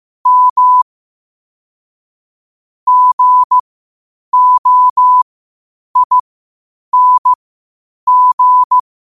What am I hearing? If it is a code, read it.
Morse code, heard as a keyed tone, about 15 wpm, spaced as for 8 wpm, 1000 Hz. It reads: M GOING
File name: qotdfiles / 8wpm